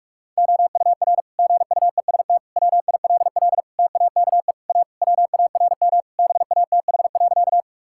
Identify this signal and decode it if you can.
Morse code, heard as a keyed tone, about 46 wpm, 700 Hz. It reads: OUR GUEST WILL TAKE A WARM BATH.
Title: Sentences from Top 1600 Words 45wpm